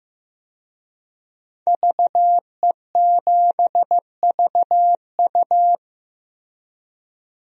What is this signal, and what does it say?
Morse code, heard as a keyed tone, about 15 wpm, 700 Hz. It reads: VE7VU
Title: DXCC-100 Call Signs - 5 Characters - Repeated 3x - 15wpm